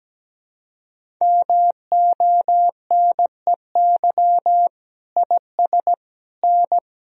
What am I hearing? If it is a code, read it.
Morse code, heard as a keyed tone, about 17 wpm, 700 Hz. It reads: MONEY IS N
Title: Sentences from Top 500 Words and Abbreviations 17wpm